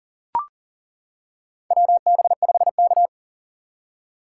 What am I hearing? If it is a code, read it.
Morse code, heard as a keyed tone, about 40 wpm, 700 Hz. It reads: WB5K
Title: K1USN SST QSO Practice - 40wpm